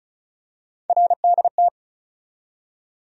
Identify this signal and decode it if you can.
Morse code, heard as a keyed tone, about 36 wpm, 700 Hz. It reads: RDT
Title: USA Call Sign Postfixes - 3 Letters 35wpm